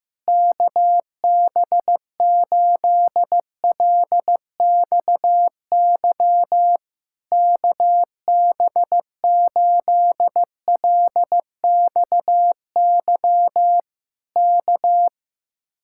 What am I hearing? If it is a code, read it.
Morse code, heard as a keyed tone, about 15 wpm, 700 Hz. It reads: KB8LXY KB8LXY K
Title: Speed-Racing Common QSO Phrases Encore - 15wpm